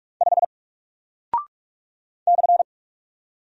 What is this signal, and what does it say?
Morse code, heard as a keyed tone, about 44 wpm, 700 Hz. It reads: 5 /
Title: Single Character (Letter:Number:Punctuation:Prosign) Alt 45wpm